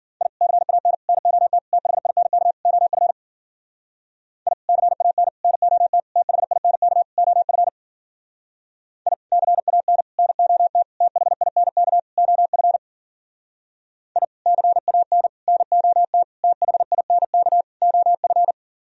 Morse code, heard as a keyed tone, about 52 wpm, 700 Hz. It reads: I CAN NOT THINK OF I CAN NOT THINK OF I CAN NOT THINK OF I CAN NOT THINK OF